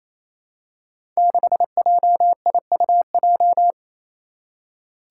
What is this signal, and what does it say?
Morse code, heard as a keyed tone, about 28 wpm, 700 Hz. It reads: 6JIUJ
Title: Five Challenging Letter-Number Combinations 28wpm